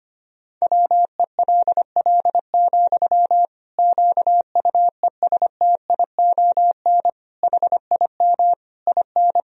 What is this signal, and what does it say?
Morse code, heard as a keyed tone, about 25 wpm, 700 Hz. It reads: WELL, QUESTION HIM IN
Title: Sentences from Top 1000 Words - Review Part 3 - 25wpm